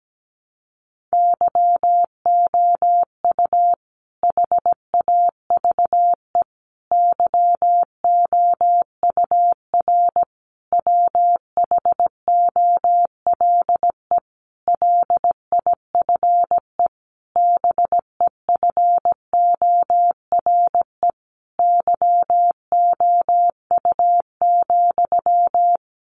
Morse code, heard as a keyed tone, about 17 wpm, 700 Hz. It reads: YOU HAVE YOUR WHOLE LIFE BEFORE YOU,
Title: Sentences from Top 400 Words 17wpm